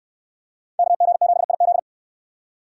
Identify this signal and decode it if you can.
Morse code, heard as a keyed tone, about 67 wpm, 700 Hz. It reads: KG7IZ